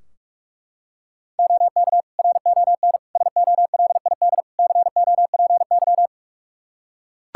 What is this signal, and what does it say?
Morse code, heard as a keyed tone, about 41 wpm, 700 Hz. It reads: OK RON SOLID COPY
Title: Practice QSO - Exchange 3 - More Info - 45wpm